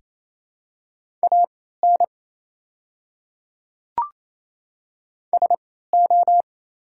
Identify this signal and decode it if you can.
Morse code, heard as a keyed tone, about 27 wpm, spaced as for 16 wpm, 690 Hz. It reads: AN SO
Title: 3x Character Spacing - Top 200 Words - 28wpm